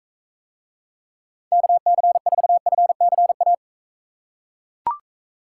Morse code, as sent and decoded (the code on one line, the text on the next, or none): -.- -.-. ....- ..-. -.-. .-
KC4FCA